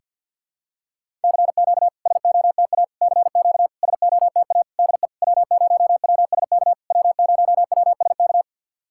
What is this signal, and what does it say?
Morse code, heard as a keyed tone, about 50 wpm, 700 Hz. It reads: CQ SOTA CQ SOTA DE W0WSK W0WSK
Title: SOTA QSO Practice 50wpm